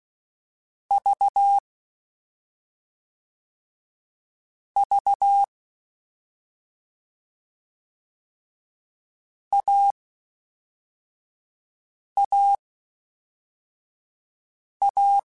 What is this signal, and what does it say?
Morse code, heard as a keyed tone, about 16 wpm, spaced as for 4 wpm, 800 Hz. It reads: V V AAA
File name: Difficult Combos / Difficult Combos-mp3s / AUV ShortRun @5